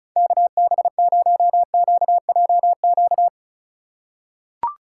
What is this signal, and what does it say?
Morse code, heard as a keyed tone, about 34 wpm, 690 Hz. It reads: KB0QJQ
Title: USA Call Signs - 6 Characters - 6x Character Spacing - 35wpm